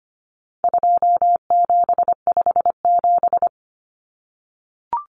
Morse code, heard as a keyed tone, about 25 wpm, 700 Hz. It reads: 2757
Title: Common Cut Numbers - 4-Digit 25wpm